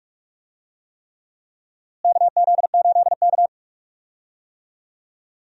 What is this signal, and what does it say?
Morse code, heard as a keyed tone, about 46 wpm, 700 Hz. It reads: KZ8K
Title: USA Call Signs - 4 Characters - Repeated 3x - 45wpm